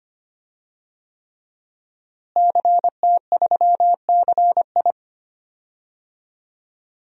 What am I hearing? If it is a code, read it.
Morse code, heard as a keyed tone, about 25 wpm, 700 Hz. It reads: CT3CI
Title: DXCC-100 Call Signs - 5 Characters - Repeated 3x - 25wpm